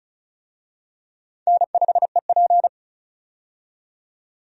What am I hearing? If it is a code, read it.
Morse code, heard as a keyed tone, about 36 wpm, 700 Hz. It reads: N5EP
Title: USA Call Signs - 4 Characters - 35wpm